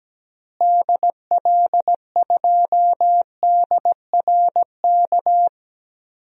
Morse code, heard as a keyed tone, about 17 wpm, 700 Hz. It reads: DL2DRK
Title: DXCC-100 Call Signs - 6 Characters - Repeated 2x - 17wpm